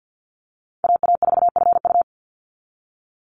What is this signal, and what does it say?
Morse code, heard as a keyed tone, about 50 wpm, 700 Hz. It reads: AA4FU
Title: Call Signs Repeated 2x - 50wpm